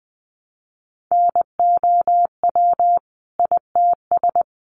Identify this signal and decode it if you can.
Morse code, heard as a keyed tone, about 19 wpm, 700 Hz. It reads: NOW ITS
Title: Sentences from Top 600 Words 20wpm